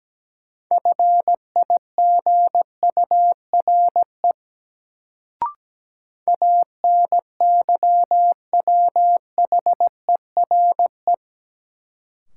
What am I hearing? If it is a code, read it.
Morse code, heard as a keyed tone, about 17 wpm, 700 Hz. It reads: FIGURE ANYWHERE
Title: Sentences from Top 1400 Words - New Word Drill Alt - 17wpm